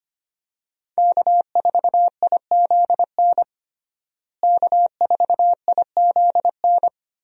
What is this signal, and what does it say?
Morse code, heard as a keyed tone, about 25 wpm, 700 Hz. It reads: K4IZN K4IZN
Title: USA Call Signs - 5 Characters - Repeated 2x - 25wpm